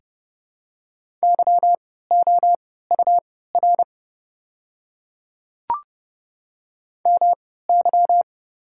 Morse code, heard as a keyed tone, about 30 wpm, spaced as for 17 wpm, 700 Hz. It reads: YOUR MY
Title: 3x Character Spacing - Top 100 Words - 30wpm